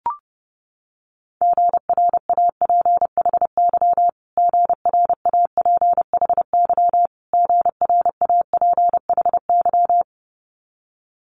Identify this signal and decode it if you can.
Morse code, heard as a keyed tone, about 30 wpm, 700 Hz. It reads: GRAPHY GRAPHY GRAPHY
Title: Multi-letter Phonemes Repeated 3x 30wpm